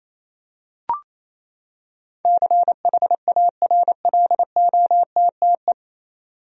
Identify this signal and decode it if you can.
Morse code, heard as a keyed tone, about 28 wpm, 700 Hz. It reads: CHARLOTTE
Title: USA Common Names - 28wpm